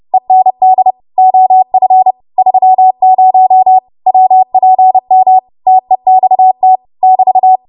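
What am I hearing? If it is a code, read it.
Morse code, heard as a keyed tone, about 30 wpm, 750 Hz. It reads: END OF 30 WPM TEXT <BT>
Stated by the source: Morse practice transmission at 30 WPM.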